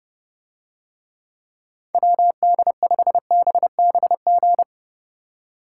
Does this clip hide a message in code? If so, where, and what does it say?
Yes: Morse code from 1.9 to 4.6 s, heard as a keyed tone, about 31 wpm, 700 Hz. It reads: WD5BBG